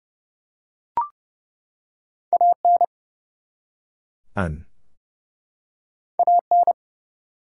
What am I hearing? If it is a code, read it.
Morse code, heard as a keyed tone, about 31 wpm, 700 Hz. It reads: AN AN
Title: Two Letter Words - 30wpm